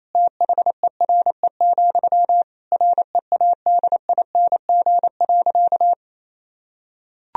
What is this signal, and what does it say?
Morse code, heard as a keyed tone, about 27 wpm, 700 Hz. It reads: THERE, READING.
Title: Sentences from Top 1000 Words - Review Part 3 - 28wpm